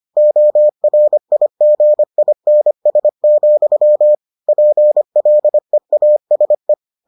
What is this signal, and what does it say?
Morse code, heard as a keyed tone, about 25 wpm, 600 Hz. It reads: ORIGINS, PLEASE